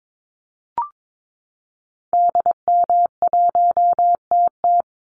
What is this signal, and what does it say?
Morse code, heard as a keyed tone, about 22 wpm, 700 Hz. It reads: DM1TT
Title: Call Signs Repeated 3x - 22wpm